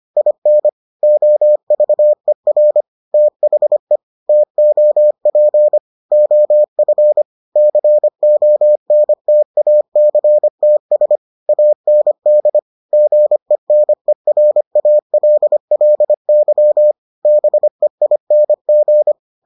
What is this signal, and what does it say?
Morse code, heard as a keyed tone, about 25 wpm, 600 Hz. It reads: IN OVER THE TOP OF CONTACTS AND GENERALLY BEING